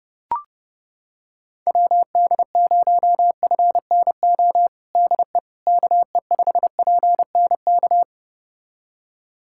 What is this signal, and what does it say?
Morse code, heard as a keyed tone, about 30 wpm, 700 Hz. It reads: WD0FNO DE KE5PNK
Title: USA Callsign DE Callsign - 6 Characters - 30wpm